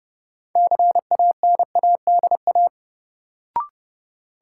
Morse code, heard as a keyed tone, about 30 wpm, 700 Hz. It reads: CANADA
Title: Sets of 5 Words - New Word Drill - 30wpm